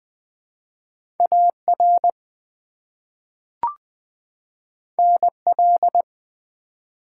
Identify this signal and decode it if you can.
Morse code, heard as a keyed tone, about 20 wpm, 700 Hz. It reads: AR NL
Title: ARRL Sections v2 - 20wpm